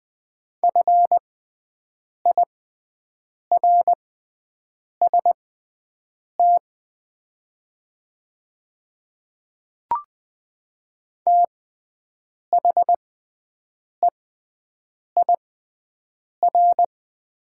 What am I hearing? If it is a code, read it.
Morse code, heard as a keyed tone, about 20 wpm, spaced as for 7 wpm, 690 Hz. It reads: FIRST THEIR
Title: 6x Character Spacing - Top 100 Words - 20wpm